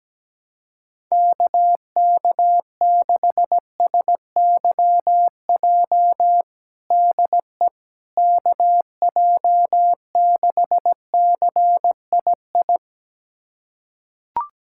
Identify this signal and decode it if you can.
Morse code, heard as a keyed tone, about 17 wpm, 700 Hz. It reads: KK6SYJ DE KJ6CII
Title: USA Callsign DE Callsign - 6 Characters - 17wpm